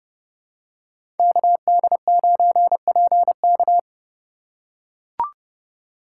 Morse code, heard as a keyed tone, about 30 wpm, 700 Hz. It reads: KD9PK
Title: USA Call Signs - 5 Characters - 3x Character Spacing - 30wpm